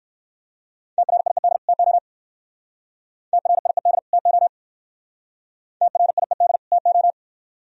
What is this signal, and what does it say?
Morse code, heard as a keyed tone, about 60 wpm, 700 Hz. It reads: TRIED TO TRIED TO TRIED TO